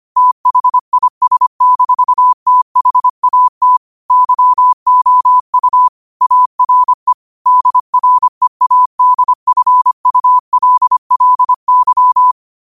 Morse code, heard as a keyed tone, about 25 wpm, 1000 Hz. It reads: THIS-THAT YOU ARE DREADFULLY